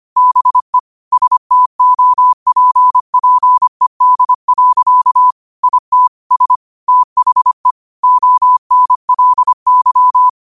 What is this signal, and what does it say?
Morse code, heard as a keyed tone, about 25 wpm, 1000 Hz. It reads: BE STOPPED. IT S THE ONLY
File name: qotdfiles / 25wpm